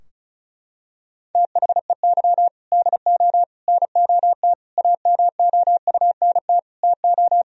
Morse code, heard as a keyed tone, about 36 wpm, 700 Hz. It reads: THEY DO NOT AMOUNT TO
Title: Sentences from Top 1000 Words - Review Part 4 - 35wpm